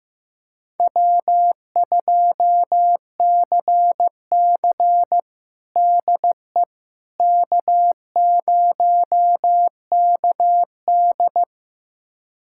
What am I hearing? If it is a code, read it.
Morse code, heard as a keyed tone, about 15 wpm, 700 Hz. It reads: W2CC DE K0KD
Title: USA Callsign DE Callsign - 4 Characters - 15wpm